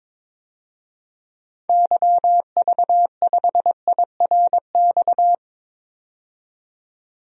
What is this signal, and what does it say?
Morse code, heard as a keyed tone, about 22 wpm, 700 Hz. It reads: YV5IRX
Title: DXCC-100 Call Signs - 6 Characters - 22wpm